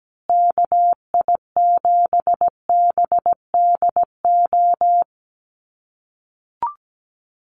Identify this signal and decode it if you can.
Morse code, heard as a keyed tone, about 17 wpm, 700 Hz. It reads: KI7BDO
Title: USA Call Signs Repeated 2x 17wpm